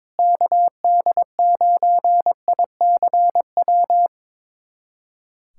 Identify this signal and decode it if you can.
Morse code, heard as a keyed tone, about 22 wpm, 700 Hz. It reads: KD9ICW
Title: USA Callsign DE Callsign 22wpm